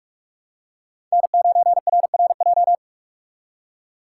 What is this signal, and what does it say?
Morse code, heard as a keyed tone, about 46 wpm, 700 Hz. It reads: N9RRJ
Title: USA Call Signs - 5 Characters - 4x Character Spacing - 45wpm